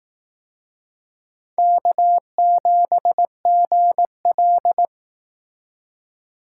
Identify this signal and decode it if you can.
Morse code, heard as a keyed tone, about 18 wpm, 700 Hz. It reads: K7GL
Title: Speed Racing - USA 1x2 Call Signs 15wpm